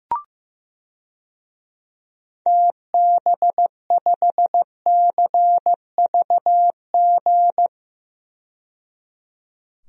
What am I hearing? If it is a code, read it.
Morse code, heard as a keyed tone, about 15 wpm, 700 Hz. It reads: TB5CVG
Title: DXCC-100 Call Signs - 6 Characters - 15wpm